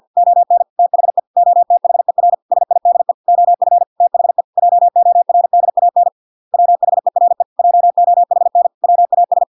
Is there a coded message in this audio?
Yes: Morse code from 0.2 to 9.4 s, heard as a keyed tone, about 50 wpm, 700 Hz. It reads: ON THE OTHER SIDE OF THE JORDAN WHERE JOHN WAS